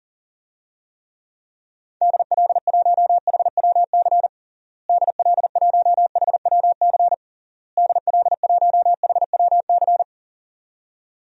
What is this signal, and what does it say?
Morse code, heard as a keyed tone, about 40 wpm, 700 Hz. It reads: DL1HWC DL1HWC DL1HWC
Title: DXCC-100 Call Signs - 6 Characters - Repeated 3x - 40wpm